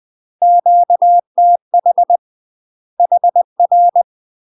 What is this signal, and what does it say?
Morse code, heard as a keyed tone, about 20 wpm, 700 Hz. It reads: QTH HR
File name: UK MASTER QSOs TRAINING / 20WPM / UK MASTER QSOs 0053